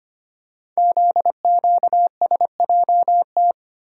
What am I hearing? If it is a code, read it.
Morse code, heard as a keyed tone, about 25 wpm, 700 Hz. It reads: ZQSJT
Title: Code Groups - 5 Letters - 25wpm